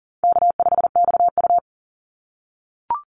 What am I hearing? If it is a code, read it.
Morse code, heard as a keyed tone, about 40 wpm, 700 Hz. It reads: K5XU
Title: Call Signs - 40wpm